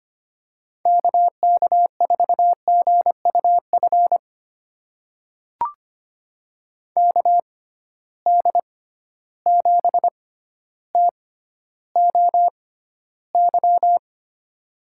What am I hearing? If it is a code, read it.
Morse code, heard as a keyed tone, about 25 wpm, spaced as for 13 wpm, 700 Hz. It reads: KK4GUF KD7TOY
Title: USA Call Signs - 6 Characters - 6x Character Spacing - 25wpm